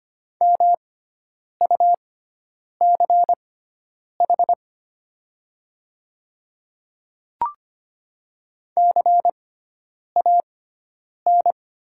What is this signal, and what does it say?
Morse code, heard as a keyed tone, about 25 wpm, spaced as for 9 wpm, 700 Hz. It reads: MUCH CAN
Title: 6x Character Spacing - Top 300 Words - 25wpm